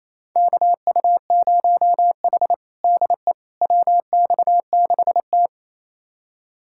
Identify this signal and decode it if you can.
Morse code, heard as a keyed tone, about 28 wpm, 700 Hz. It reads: KU0H DE WX6T
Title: USA Callsign DE Callsign - 4 Characters - 28wpm